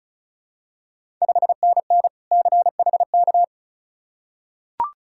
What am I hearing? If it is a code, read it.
Morse code, heard as a keyed tone, about 36 wpm, 700 Hz. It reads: 5NN CHK
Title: Ontario QSO Party Exchange 35wpm